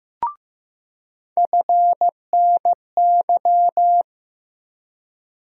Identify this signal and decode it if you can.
Morse code, heard as a keyed tone, about 15 wpm, 700 Hz. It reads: FNY
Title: USA Call Sign Postfixes - 3 Letters 15wpm